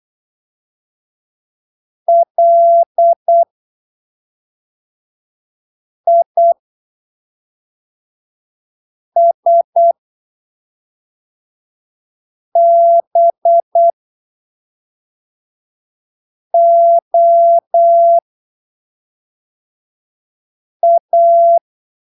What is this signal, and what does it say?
Morse code, heard as a keyed tone, about 8 wpm, spaced as for 3 wpm, 680 Hz. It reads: LISBOA
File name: arquivos / texto / texto 16 08ppm